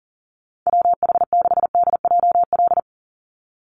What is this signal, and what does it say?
Morse code, heard as a keyed tone, about 40 wpm, 700 Hz. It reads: WH6DJL
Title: USA Call Signs 40wpm